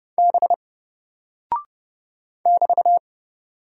Morse code, heard as a keyed tone, about 31 wpm, 700 Hz. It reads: B <BT>
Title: Single Character (Letter:Number:Punctuation:Prosign) 30wpm